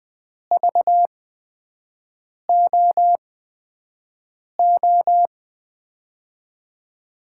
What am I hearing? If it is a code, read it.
Morse code, heard as a keyed tone, about 20 wpm, spaced as for 5 wpm, 700 Hz. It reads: VOO
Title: USA Call Signs - 6 Characters - 8x Character Spacing - 20wpm